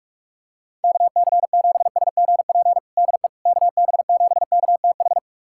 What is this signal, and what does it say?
Morse code, heard as a keyed tone, about 46 wpm, 700 Hz. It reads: KC7SGP DE KB7KTH
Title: USA Callsign DE Callsign 45wpm